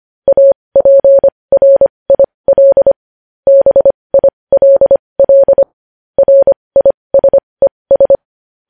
Morse code, heard as a keyed tone, about 25 wpm, spaced as for 20 wpm, 550 Hz. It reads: APRIL BILL RISES
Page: News Headlines in Morse Code at 20 WPM